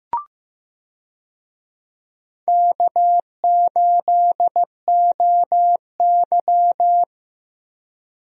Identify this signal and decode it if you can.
Morse code, heard as a keyed tone, about 15 wpm, 700 Hz. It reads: K8OY
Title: USA Call Signs - 4 Characters - Repeated 3x - 15wpm